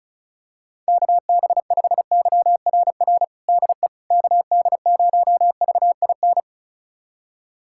Morse code, heard as a keyed tone, about 34 wpm, 700 Hz. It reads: KB5YRR DE KD0VIN
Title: USA Callsign DE Callsign 35wpm